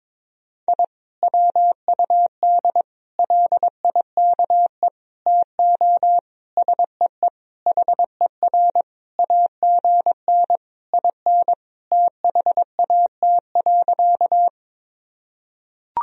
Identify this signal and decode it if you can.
Morse code, heard as a keyed tone, about 22 wpm, 700 Hz. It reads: I WUD LIKE TO SEE HER AGN IN THAT.
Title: Sentences from Top 200 Words and Abbreviations 22wpm